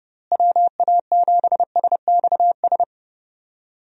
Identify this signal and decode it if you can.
Morse code, heard as a keyed tone, about 30 wpm, 700 Hz. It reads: WA7SXS
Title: USA Call Signs - 6 Characters - Repeated 3x - 30wpm